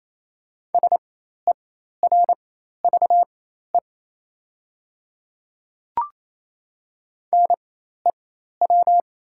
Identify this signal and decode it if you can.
Morse code, heard as a keyed tone, about 28 wpm, spaced as for 13 wpm, 700 Hz. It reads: SERVE NEW
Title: 4x Character Spacing - Top 500 Words - 28wpm